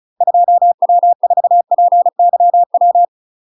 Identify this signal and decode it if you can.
Morse code, heard as a keyed tone, about 34 wpm, 700 Hz. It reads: 2W4PYW
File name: UK MASTER QSOs TRAINING / 35WPM / UK MASTER QSOs 0008